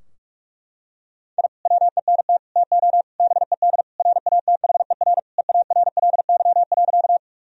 Morse code, heard as a keyed tone, about 46 wpm, 700 Hz. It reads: I WENT TO BED RATHER EARLY.
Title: Sentences from Top 1000 Words - Review Part 4 - 45wpm